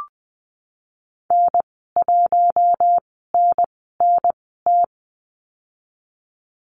Morse code, heard as a keyed tone, about 20 wpm, spaced as for 15 wpm, 700 Hz. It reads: N1NNT
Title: 055 Lesson 10g - Letter r - Call Signs 20wpm